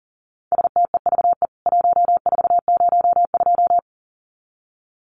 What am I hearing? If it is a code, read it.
Morse code, heard as a keyed tone, about 40 wpm, 700 Hz. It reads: STEVE 1402
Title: CWT Exchange 40wpm